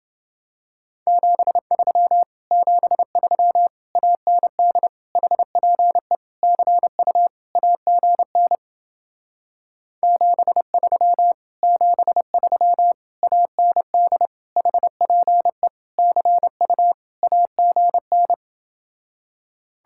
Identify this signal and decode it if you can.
Morse code, heard as a keyed tone, about 29 wpm, 700 Hz. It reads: 73 73 AND HPE CU AGN 73 73 AND HPE CU AGN